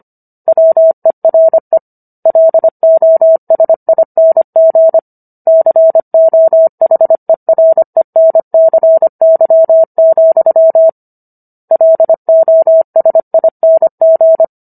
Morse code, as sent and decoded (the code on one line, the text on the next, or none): .-- . .-. / .-.. --- ... .. -. --. / -.-. --- .... .-. -. -.-. -.-- --..-- / .-.. --- ... .. -. --.
WERE LOSING COHERENCY, LOSING